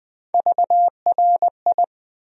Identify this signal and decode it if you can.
Morse code, heard as a keyed tone, about 20 wpm, 700 Hz. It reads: VRI
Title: USA Call Sign Postfixes - 3 Letters 20wpm